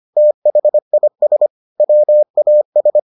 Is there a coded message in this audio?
Yes: Morse code from 0.2 to 3.0 s, heard as a keyed tone, about 25 wpm, 600 Hz. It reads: THIS WAS